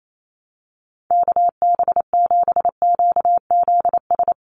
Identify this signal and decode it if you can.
Morse code, heard as a keyed tone, about 28 wpm, 700 Hz. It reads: KB7QZS
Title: USA Call Signs 28wpm